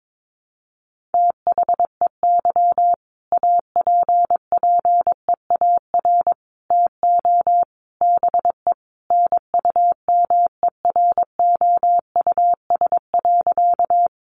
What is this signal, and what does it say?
Morse code, heard as a keyed tone, about 22 wpm, 700 Hz. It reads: THEY APPEAR TO BE NUMEROUS.
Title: Sentences from Top 1400 Words 22wpm